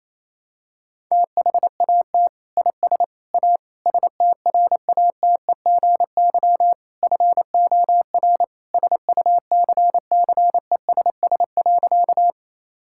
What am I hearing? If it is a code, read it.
Morse code, heard as a keyed tone, about 27 wpm, 700 Hz. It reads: THAT IS A STRATEGY FOR SUCCESS.
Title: Sentences from Top 1000 Words - Review Part 5 - 28wpm